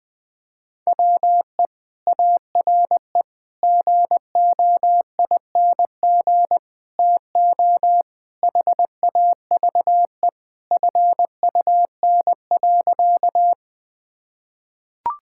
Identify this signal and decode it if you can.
Morse code, heard as a keyed tone, about 20 wpm, 700 Hz. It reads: WE ARE GOING TO HAVE FUN.
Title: Sentences with 4-Word Phrases - 20wpm